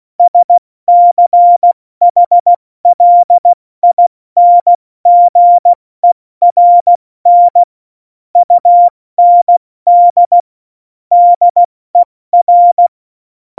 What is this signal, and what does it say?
Morse code, heard as a keyed tone, about 16 wpm, 700 Hz. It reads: SCHLINGERN UND DER